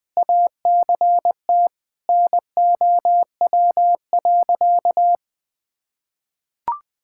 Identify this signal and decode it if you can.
Morse code, heard as a keyed tone, about 20 wpm, 700 Hz. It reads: ACT NOW.
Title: Sentences with 6-Word Phrases - 20wpm